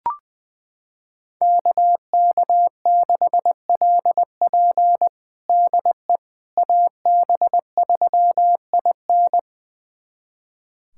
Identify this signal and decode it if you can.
Morse code, heard as a keyed tone, about 20 wpm, 700 Hz. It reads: KK6LP DE AB3IN
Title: USA Callsign DE Callsign - 5 Characters - 20wpm